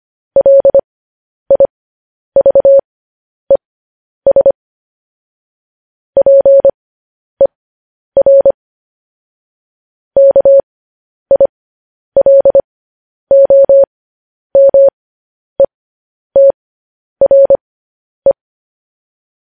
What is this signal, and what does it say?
Morse code, heard as a keyed tone, about 25 wpm, spaced as for 10 wpm, 550 Hz. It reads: LIVES PER KILOMETRE
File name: podcast / News headlines-1x-25x10